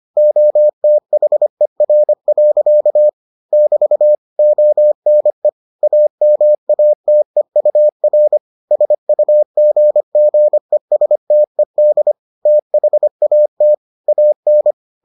Morse code, heard as a keyed tone, about 25 wpm, 600 Hz. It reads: OTHER. <BT> ONE AMATEUR SUGGESTED THAT AN